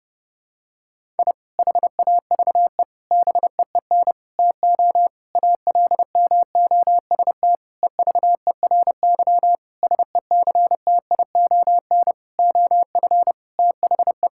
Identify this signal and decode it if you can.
Morse code, heard as a keyed tone, about 32 wpm, 700 Hz. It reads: I HAVE BEEN TO ALMOST EVERY SECTION OF THE
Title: Sentences from Top 1000 Words - Review Part 5 - 30wpm